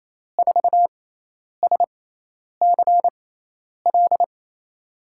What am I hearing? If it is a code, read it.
Morse code, heard as a keyed tone, about 28 wpm, spaced as for 10 wpm, 700 Hz. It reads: 4SCL
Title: USA Call Signs - 6 Characters - 6x Character Spacing - 28wpm